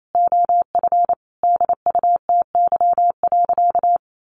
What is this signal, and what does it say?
Morse code, heard as a keyed tone, about 28 wpm, 700 Hz. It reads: OF DUTY.
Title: Sentences from Top 1500 Words 28wpm